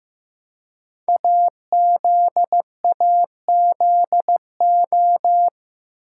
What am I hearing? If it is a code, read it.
Morse code, heard as a keyed tone, about 15 wpm, 700 Hz. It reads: AZAZO
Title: Code Groups - 5 Letters - 15wpm